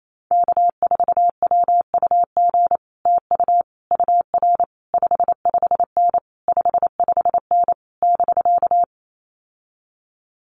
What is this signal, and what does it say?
Morse code, heard as a keyed tone, about 28 wpm, 700 Hz. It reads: K4WUG TU UR 55N 55N <BK>
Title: Common QSO Phrases Encore 28wpm